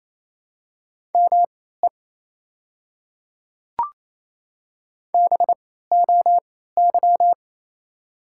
Morse code, heard as a keyed tone, about 28 wpm, spaced as for 16 wpm, 700 Hz. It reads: ME BOY
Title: 3x Character Spacing - Top 200 Words - 28wpm